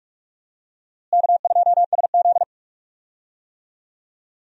Morse code, heard as a keyed tone, about 46 wpm, 700 Hz. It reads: K2SZ
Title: USA Call Signs - 4 Characters - Repeated 2x - 45wpm